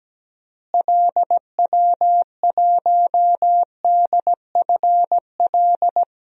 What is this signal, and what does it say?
Morse code, heard as a keyed tone, about 17 wpm, 700 Hz. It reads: LW1DFL
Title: DXCC-100 Call Signs - 6 Characters - Repeated 3x - 17wpm